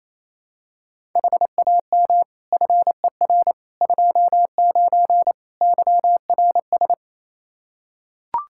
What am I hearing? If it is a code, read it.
Morse code, heard as a keyed tone, about 28 wpm, 700 Hz. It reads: HAM FER 29 YRS
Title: Practice QSO - Exchange 3 - More Info - 28wpm